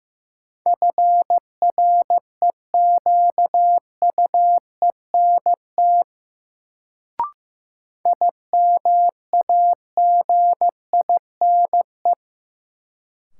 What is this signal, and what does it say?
Morse code, heard as a keyed tone, about 15 wpm, 700 Hz. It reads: FREQUENT IMAGINE
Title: Sentences from Top 1400 Words - New Word Drill - 15wpm